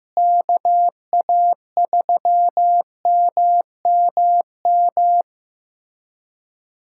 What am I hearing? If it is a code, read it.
Morse code, heard as a keyed tone, about 15 wpm, 700 Hz. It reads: KA3MMM
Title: USA Callsign DE Callsign 15wpm